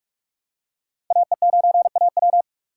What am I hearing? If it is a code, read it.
Morse code, heard as a keyed tone, about 44 wpm, 700 Hz. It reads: AE9AW